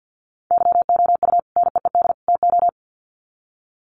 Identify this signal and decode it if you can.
Morse code, heard as a keyed tone, about 50 wpm, 700 Hz. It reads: YOU NEED TO
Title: Sets of 3 Words 50wpm